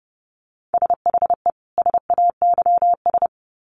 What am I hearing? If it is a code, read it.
Morse code, heard as a keyed tone, about 30 wpm, 700 Hz. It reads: SHE SAYS
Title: Sets of 2 Words 30wpm